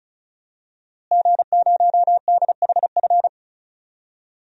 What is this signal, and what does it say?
Morse code, heard as a keyed tone, about 36 wpm, 700 Hz. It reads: G0DHF
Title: DXCC-100 Call Signs - 5 Characters - Repeated 2x - 35wpm